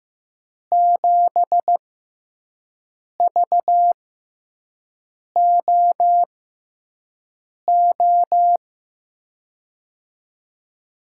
Morse code, heard as a keyed tone, about 15 wpm, spaced as for 5 wpm, 700 Hz. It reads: 7VOO
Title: USA Call Signs - 6x Character Spacing - 15wpm